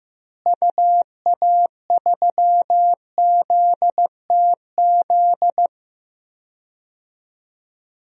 Morse code, heard as a keyed tone, about 15 wpm, 700 Hz. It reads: UA3ZTZ
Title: DXCC-100 Call Signs - 6 Characters - Repeated 2x - 15wpm